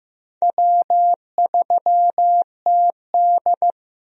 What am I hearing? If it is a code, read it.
Morse code, heard as a keyed tone, about 15 wpm, 700 Hz. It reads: W3TD
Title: USA Call Signs - 4 Characters - Repeated 3x - 15wpm